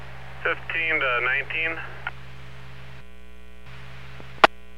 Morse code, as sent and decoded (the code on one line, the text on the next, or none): none